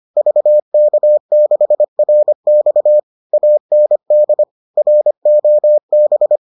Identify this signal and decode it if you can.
Morse code, heard as a keyed tone, about 25 wpm, 600 Hz. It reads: VK6RX AND ROB